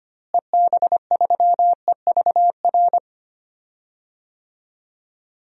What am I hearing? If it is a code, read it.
Morse code, heard as a keyed tone, about 25 wpm, 700 Hz. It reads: EB3EVR
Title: DXCC-100 Call Signs - 6 Characters - Repeated 3x - 25wpm